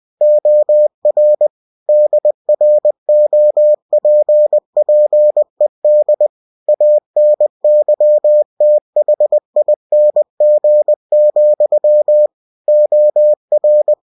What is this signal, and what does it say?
Morse code, heard as a keyed tone, about 20 wpm, 600 Hz. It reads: OR DROPPED ANYTHING, OR
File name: cw-books / winnie-the-pooh / winnie-20wpm-page-0106